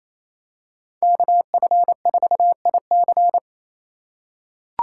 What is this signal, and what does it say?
Morse code, heard as a keyed tone, about 28 wpm, 700 Hz. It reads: KF4IC
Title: USA Call Signs - 4x Character Spacing - 28wpm